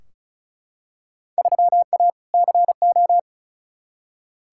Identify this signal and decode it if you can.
Morse code, heard as a keyed tone, about 34 wpm, 700 Hz. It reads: 3A CO
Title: Field Day Exchanges Encore - 35wpm